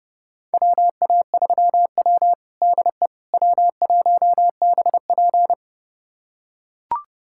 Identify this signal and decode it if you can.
Morse code, heard as a keyed tone, about 32 wpm, 700 Hz. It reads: WA3W DE W1BP
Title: USA Callsign DE Callsign - 4 Characters - 30wpm